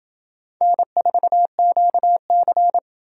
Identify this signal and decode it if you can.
Morse code, heard as a keyed tone, about 29 wpm, 700 Hz. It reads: N4QC